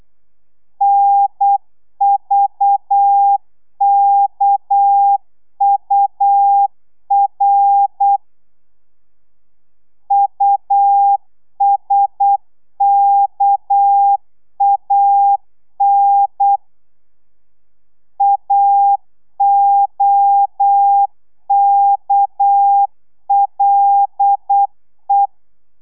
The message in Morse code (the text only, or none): NVKUR USKAN AOKLE